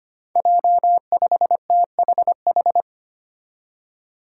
Morse code, heard as a keyed tone, about 25 wpm, 700 Hz. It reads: J5THH
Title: Five Challenging Letter-Number Combinations 25wpm